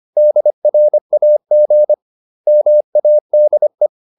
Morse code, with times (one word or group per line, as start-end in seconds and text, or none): 0.2-1.9 DRAG
2.5-3.9 MADE